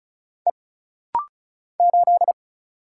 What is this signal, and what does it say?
Morse code, heard as a keyed tone, about 33 wpm, 700 Hz. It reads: E 8
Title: Single Letter-Number Alt - Rapid-Fire 35wpm